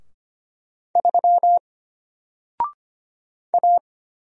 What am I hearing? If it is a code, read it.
Morse code, heard as a keyed tone, about 25 wpm, 700 Hz. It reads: 3 A
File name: Single-Letter-Number-25wpm